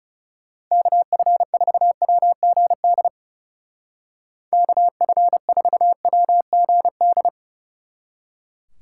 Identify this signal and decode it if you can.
Morse code, heard as a keyed tone, about 40 wpm, 700 Hz. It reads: KF4WGD KF4WGD